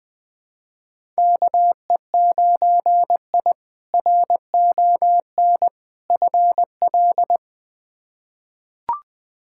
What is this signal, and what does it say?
Morse code, heard as a keyed tone, about 20 wpm, 700 Hz. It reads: KE9I RON FL
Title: K1USN SST QSO Practice - 20wpm